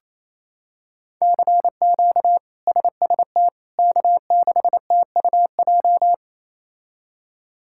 Morse code, heard as a keyed tone, about 28 wpm, 700 Hz. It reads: CQ SST K6TUJ
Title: K1USN SST QSO Practice - 28wpm